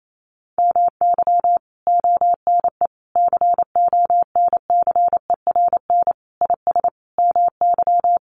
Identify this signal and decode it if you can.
Morse code, heard as a keyed tone, about 28 wpm, 700 Hz. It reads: MY ONE CONCERN IS MY
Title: Sentences from Top 1200 Words 28wpm